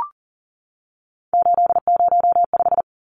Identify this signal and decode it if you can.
Morse code, heard as a keyed tone, about 40 wpm, 700 Hz. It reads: 805
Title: Numbers - 3-Digit 40wpm